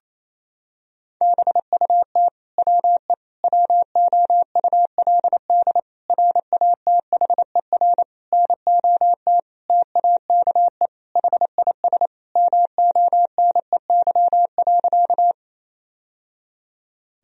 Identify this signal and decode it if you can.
Morse code, heard as a keyed tone, about 27 wpm, 700 Hz. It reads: BUT WE WOULD RATHER NOT TAKE HIS MONEY.
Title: Sentences from Top 1000 Words - Review Part 4 - 28wpm